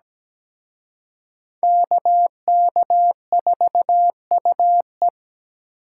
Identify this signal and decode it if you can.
Morse code, heard as a keyed tone, about 20 wpm, 700 Hz. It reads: KK4UE
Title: Speed Racing - USA 2x2 Call Signs 17wpm